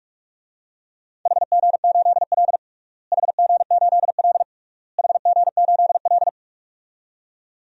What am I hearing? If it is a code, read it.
Morse code, heard as a keyed tone, about 46 wpm, 700 Hz. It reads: HG8L HG8L HG8L
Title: DXCC-100 Call Signs - 4 Characters - Repeated 3x - 45wpm